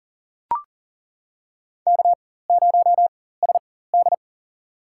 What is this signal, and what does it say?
Morse code, heard as a keyed tone, about 40 wpm, spaced as for 19 wpm, 700 Hz. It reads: K0SD
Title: USA Call Signs - 4 Characters - 4x Character Spacing - 40wpm